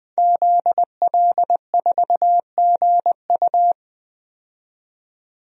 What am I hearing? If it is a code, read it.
Morse code, heard as a keyed tone, about 20 wpm, 700 Hz. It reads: ZL4GU
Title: DXCC-100 Call Signs - 5 Characters - 20wpm